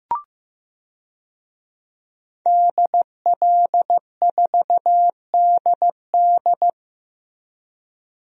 Morse code, heard as a keyed tone, about 15 wpm, 700 Hz. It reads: DL4DD
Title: DXCC-100 Call Signs - 5 Characters - Repeated 2x - 15wpm